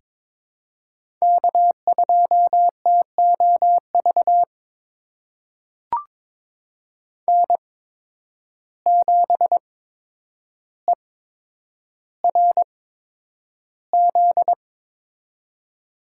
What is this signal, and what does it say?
Morse code, heard as a keyed tone, about 22 wpm, spaced as for 9 wpm, 700 Hz. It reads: K2TOV N7ERZ
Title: USA Call Signs - 5 Characters - 8x Character Spacing - 22wpm